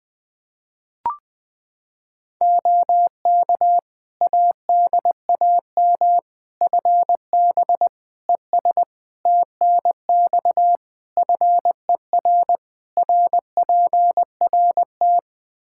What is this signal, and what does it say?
Morse code, heard as a keyed tone, about 21 wpm, 700 Hz. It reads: OK ADAM FB ES TNX FER RPRT
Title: Practice QSO - Exchange 2 - Info - 20wpm